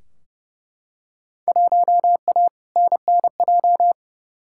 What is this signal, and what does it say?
Morse code, heard as a keyed tone, about 29 wpm, 700 Hz. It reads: 1A NNJ
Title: Field Day Exchanges Encore - 30wpm